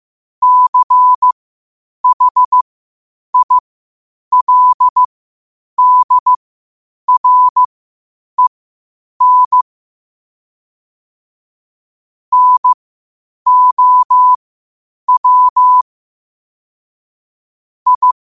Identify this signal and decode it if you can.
Morse code, heard as a keyed tone, about 15 wpm, spaced as for 8 wpm, 1000 Hz. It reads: CHILDREN NOW I
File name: qotdfiles / 8wpm